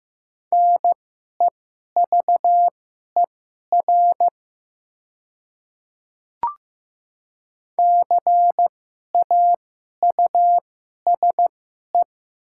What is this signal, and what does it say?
Morse code, heard as a keyed tone, about 15 wpm, spaced as for 11 wpm, 700 Hz. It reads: NEVER CAUSE
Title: 2x Character Spacing - Top 300 Words - 15wpm